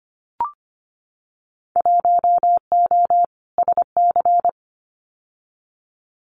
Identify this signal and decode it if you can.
Morse code, heard as a keyed tone, about 25 wpm, 700 Hz. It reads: 1O SC
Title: Winter Field Day Exchanges 25wpm